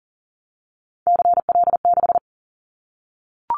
Morse code, heard as a keyed tone, about 40 wpm, 700 Hz. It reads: CL6
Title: Callsign Prefixes for Top 100 DXCC - 40wpm